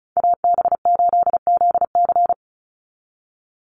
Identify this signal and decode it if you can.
Morse code, heard as a keyed tone, about 34 wpm, 700 Hz. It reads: AB8ZC
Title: USA Call Signs 35wpm